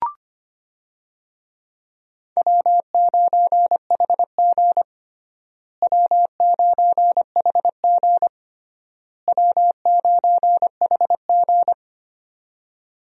Morse code, heard as a keyed tone, about 25 wpm, 700 Hz. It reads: W9HG W9HG W9HG
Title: USA Call Signs - 4 Characters - Repeated 3x - 25wpm